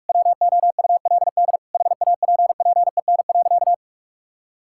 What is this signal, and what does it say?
Morse code, heard as a keyed tone, about 46 wpm, 700 Hz. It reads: WOULD HAPPEN.
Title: Sentences from Top 1000 Words - Review Part 6 - 45wpm